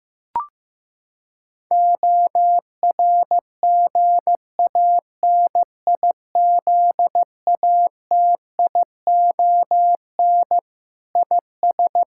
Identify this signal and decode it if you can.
Morse code, heard as a keyed tone, about 15 wpm, 700 Hz. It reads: ORGANIZATION IS
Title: Sentences from Top 1000 Words - Review Part 5 - 15wpm